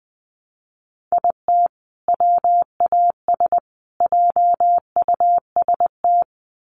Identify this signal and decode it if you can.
Morse code, heard as a keyed tone, about 20 wpm, 700 Hz. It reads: IT WAS JUST
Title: Sets of 3 Words 20wpm